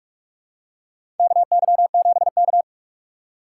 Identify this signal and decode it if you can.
Morse code, heard as a keyed tone, about 42 wpm, 700 Hz. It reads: KY7K
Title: K1USN SST QSO Practice - 45wpm